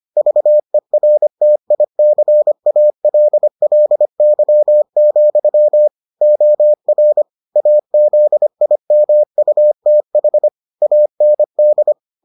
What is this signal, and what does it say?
Morse code, heard as a keyed tone, about 25 wpm, 600 Hz. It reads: VERTICALLY, OR AZIMUTH AND